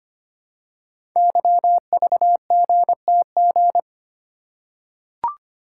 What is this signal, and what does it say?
Morse code, heard as a keyed tone, about 25 wpm, 700 Hz. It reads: YVGTG
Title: Code Groups - 5 Letters - 25wpm